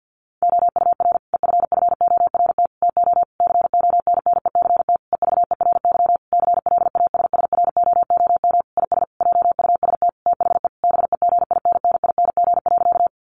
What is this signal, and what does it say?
Morse code, heard as a keyed tone, about 50 wpm, 700 Hz. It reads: OUR EFFORT TO CONNECT EVERY CLASSROOM IS JUST THE BEGINNING.
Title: Sentences from Top 1500 Words 50wpm